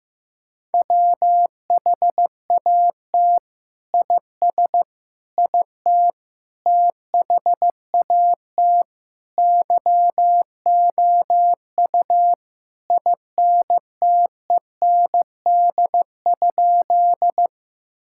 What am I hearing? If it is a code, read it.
Morse code, heard as a keyed tone, about 16 wpm, 700 Hz. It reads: WHAT IS IT THAT YOU INTEND?
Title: Sentences from Top 1600 Words 15wpm